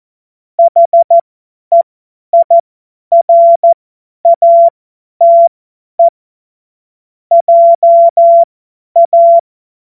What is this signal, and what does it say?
Morse code, heard as a keyed tone, about 14 wpm, spaced as for 10 wpm, 680 Hz. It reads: HEIRATE JA
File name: Morsekurs / Witze / Witz 08. Februar 2022 50 F70